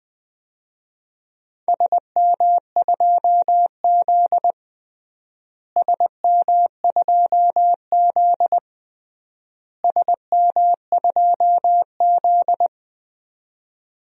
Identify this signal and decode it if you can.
Morse code, heard as a keyed tone, about 20 wpm, 700 Hz. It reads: SM2Z SM2Z SM2Z
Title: DXCC-100 Call Signs - 4 Characters - Repeated 3x - 20wpm